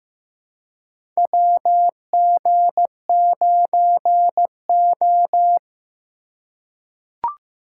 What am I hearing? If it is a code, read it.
Morse code, heard as a keyed tone, about 15 wpm, 700 Hz. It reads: WG9O
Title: USA Call Signs - 4 Characters - 3x Character Spacing - 15wpm